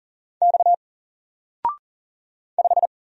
Morse code, heard as a keyed tone, about 40 wpm, 700 Hz. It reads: X 5
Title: Single Letter-Number Alt 40wpm